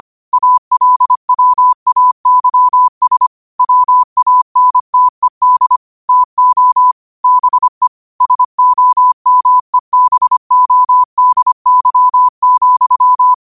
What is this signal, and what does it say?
Morse code, heard as a keyed tone, about 25 wpm, 1000 Hz. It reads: ALWAYS WANTED TO BE SOMEBODY,
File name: qotdfiles / 25wpm